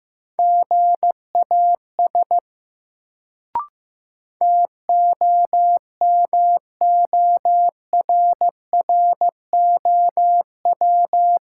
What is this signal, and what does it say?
Morse code, heard as a keyed tone, about 15 wpm, 700 Hz. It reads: GAS TOMORROW
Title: Sentences from Top 1100 Words - New Word Drill - 15wpm